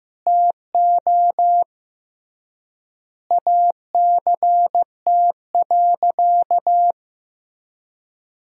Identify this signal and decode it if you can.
Morse code, heard as a keyed tone, about 16 wpm, 700 Hz. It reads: TO ACT.
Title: 3x Word-Spacing - Sentences from Top 200 Words - 15wpm